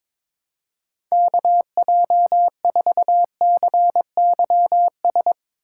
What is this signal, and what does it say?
Morse code, heard as a keyed tone, about 21 wpm, 700 Hz. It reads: KJ4CYS
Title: USA Call Signs - 2x Character Spacing - 22wpm